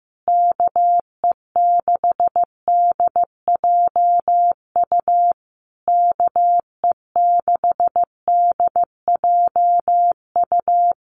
Morse code, heard as a keyed tone, about 15 wpm, 700 Hz. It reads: KE6DJU KE6DJU
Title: Common QSO Phrases Encore 15wpm